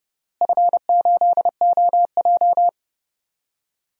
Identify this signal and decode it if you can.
Morse code, heard as a keyed tone, about 31 wpm, 700 Hz. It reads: F8OJ
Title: DXCC-100 Call Signs - 4 Characters - Repeated 3x - 30wpm